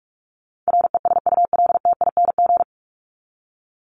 Morse code, heard as a keyed tone, about 44 wpm, 700 Hz. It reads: RESULTING
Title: ING Suffix - Top 100 Words with Base Words 45wpm